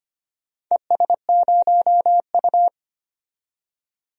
Morse code, heard as a keyed tone, about 25 wpm, 700 Hz. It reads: ES0U
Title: DXCC-100 Call Signs - 4 Characters - 25wpm